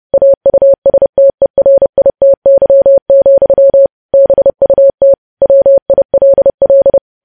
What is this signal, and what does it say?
Morse code, heard as a keyed tone, about 30 wpm, 550 Hz. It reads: AUSTERITY, BUT WILL
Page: news headlines in morse code at 30 wpm